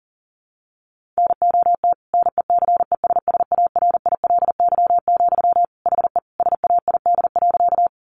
Morse code, heard as a keyed tone, about 40 wpm, 700 Hz. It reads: NOT NECESSARILY, HE SAID.
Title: Sentences from Top 1400 Words 40wpm